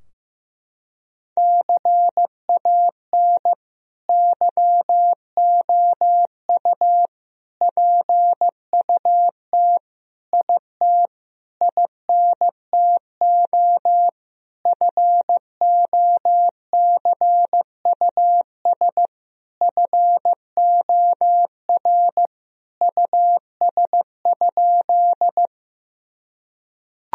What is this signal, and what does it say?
Morse code, heard as a keyed tone, about 16 wpm, 700 Hz. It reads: CAN YOU PUT IT INTO FOCUS FOR US?
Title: Sentences from Top 1000 Words - Review Part 4 - 15wpm